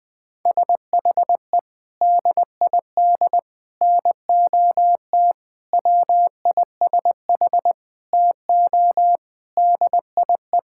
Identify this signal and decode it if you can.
Morse code, heard as a keyed tone, about 20 wpm, 700 Hz. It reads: SHE DID NOT WISH TO DIE
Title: Sentences from Top 1600 Words 20wpm